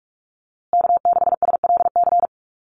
Morse code, heard as a keyed tone, about 44 wpm, 700 Hz. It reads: K6SLC
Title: USA Call Signs Repeated 2x 45wpm